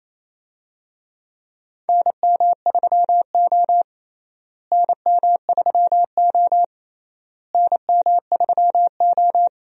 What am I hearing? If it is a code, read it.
Morse code, heard as a keyed tone, about 28 wpm, 700 Hz. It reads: NM3O NM3O NM3O
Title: USA Call Signs - 4 Characters - Repeated 3x - 28wpm